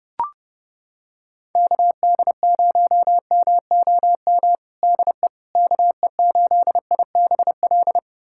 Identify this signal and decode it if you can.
Morse code, heard as a keyed tone, about 30 wpm, 700 Hz. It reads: KD0MOM DE KE8IBL
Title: USA Callsign DE Callsign - 6 Characters - 30wpm